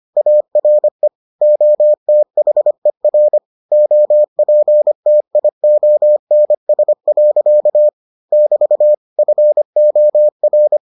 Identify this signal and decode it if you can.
Morse code, heard as a keyed tone, about 25 wpm, 600 Hz. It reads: ARE OTHER OPTIONS. <BT> FOR